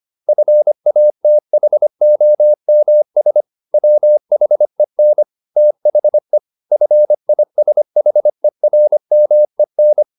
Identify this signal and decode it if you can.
Morse code, heard as a keyed tone, about 25 wpm, 600 Hz. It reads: FATHOMS WHEN THE FISHERMEN
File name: the old man and the sea part 1 25wpm